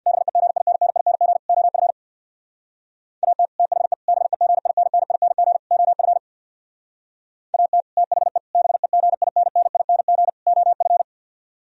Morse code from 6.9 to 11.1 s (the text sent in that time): AT THE BEGINNING OF